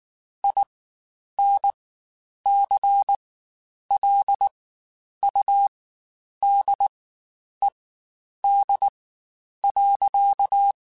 Morse code, heard as a keyed tone, about 19 wpm, spaced as for 9 wpm, 800 Hz. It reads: INCLUDED.
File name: Quotes Spaced / Quotes Spaced-mp3s / Airtight sp @18